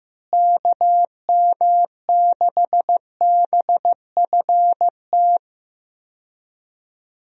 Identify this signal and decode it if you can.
Morse code, heard as a keyed tone, about 16 wpm, 700 Hz. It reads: KM6BFT
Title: USA Call Signs - 6 Characters - Repeated 3x - 15wpm